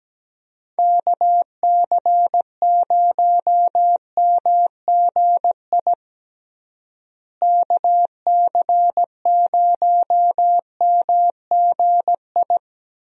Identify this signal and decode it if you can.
Morse code, heard as a keyed tone, about 17 wpm, 700 Hz. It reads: KC0MGI KC0MGI
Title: USA Call Signs - 6 Characters - Repeated 3x - 17wpm